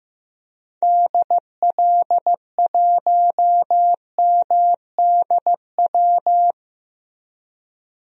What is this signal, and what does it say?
Morse code, heard as a keyed tone, about 16 wpm, 700 Hz. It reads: DL1MDW
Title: DXCC-100 Call Signs - 6 Characters - Repeated 2x - 15wpm